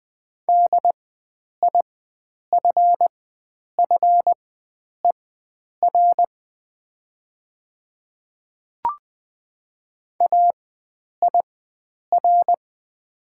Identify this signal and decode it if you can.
Morse code, heard as a keyed tone, about 20 wpm, spaced as for 9 wpm, 700 Hz. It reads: DIFFER AIR
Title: 4x Character Spacing - Top 300 Words - 20wpm